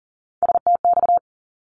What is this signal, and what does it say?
Morse code, heard as a keyed tone, about 42 wpm, 700 Hz. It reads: STX
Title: ARRL Sections 40wpm